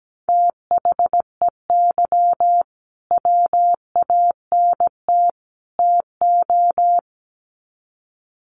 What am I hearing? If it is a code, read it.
Morse code, heard as a keyed tone, about 17 wpm, 700 Hz. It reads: THEY WANT TO
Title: Sets of 3 Words 17wpm